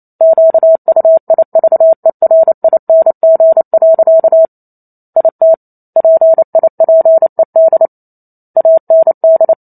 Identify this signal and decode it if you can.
Morse code, heard as a keyed tone, about 28 wpm, 650 Hz. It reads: QUIVERING. IT PIPED AND
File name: Chapter 2 - 28 WPM